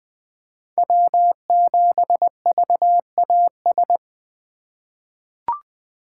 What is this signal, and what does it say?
Morse code, heard as a keyed tone, about 21 wpm, 700 Hz. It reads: W7VAS
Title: K1USN SST QSO Practice - 20wpm